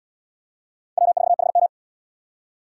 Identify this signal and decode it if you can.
Morse code, heard as a keyed tone, about 78 wpm, 700 Hz. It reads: W4HM